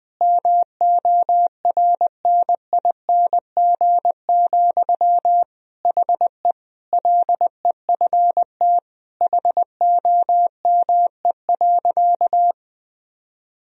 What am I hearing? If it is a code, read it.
Morse code, heard as a keyed tone, about 20 wpm, 700 Hz. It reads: MORNING, HE LEFT HOME.
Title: Sentences from Top 1000 Words - Review Part 4 - 20wpm